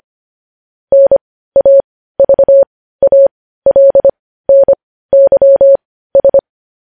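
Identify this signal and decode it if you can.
Morse code, heard as a keyed tone, about 25 wpm, spaced as for 15 wpm, 550 Hz. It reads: NAVALNYS
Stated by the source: Morse code news headlines at 15 wpm.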